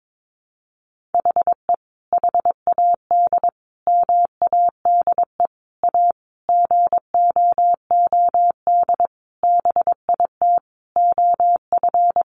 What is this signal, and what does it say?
Morse code, heard as a keyed tone, about 22 wpm, 700 Hz. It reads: HE HAD MADE A GOOD BIT OF
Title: Sentences from Top 900 Words 22wpm